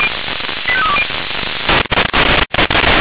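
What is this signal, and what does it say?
Five-tone selective call (ZVEI), heard as quick runs of five tones, tones of about 70 ms. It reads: 85320
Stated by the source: ZVEI five-tone call received by SDR, fire brigade Orlamünde (feuerwehr-orlamuende) audio archive.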